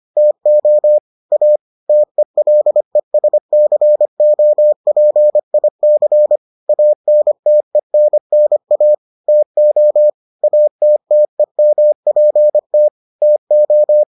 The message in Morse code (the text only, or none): TO A TELESCOPIC ANTENNA TO ATTEMPT TO